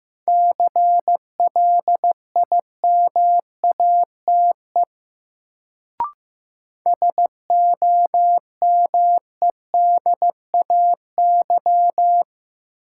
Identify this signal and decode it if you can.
Morse code, heard as a keyed tone, about 15 wpm, 700 Hz. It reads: CLIMATE SOMEDAY
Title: Sentences from Top 1600 Words - New Word Drill Alt - 15wpm